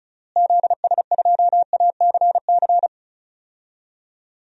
Morse code, heard as a keyed tone, about 36 wpm, 700 Hz. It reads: ZS2ACC
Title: DXCC-100 Call Signs - 6 Characters - Repeated 3x - 35wpm